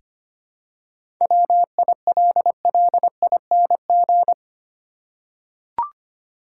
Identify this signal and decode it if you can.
Morse code, heard as a keyed tone, about 25 wpm, 700 Hz. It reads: WILLING
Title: Seven Letter Words - 25wpm